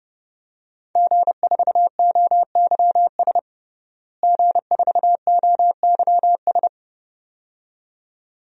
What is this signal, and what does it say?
Morse code, heard as a keyed tone, about 32 wpm, 700 Hz. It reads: G4OYS G4OYS
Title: DXCC-100 Call Signs - 5 Characters - Repeated 2x - 30wpm